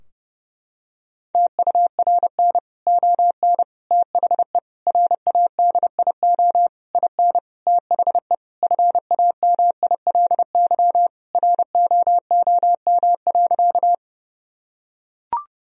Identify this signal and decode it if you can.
Morse code, heard as a keyed tone, about 32 wpm, 700 Hz. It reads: TURN ON THE RADIO IN THE FAMILY ROOM.
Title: Sentences from Top 1000 Words - Review Part 3 - 30wpm